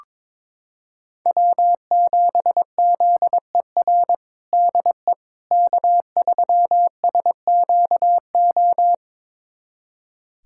Morse code, heard as a keyed tone, about 22 wpm, 700 Hz. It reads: W7ZER DE K3SQO
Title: USA Callsign DE Callsign - 5 Characters - 22wpm